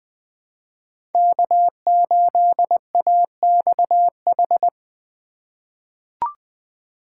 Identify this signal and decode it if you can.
Morse code, heard as a keyed tone, about 20 wpm, 700 Hz. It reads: K8AXH
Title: USA Call Signs - 5 Characters - 4x Character Spacing - 20wpm